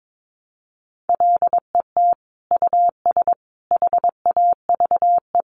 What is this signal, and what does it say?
Morse code, heard as a keyed tone, about 22 wpm, 700 Hz. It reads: LET US HAVE
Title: Sentences from Top 900 Words 22wpm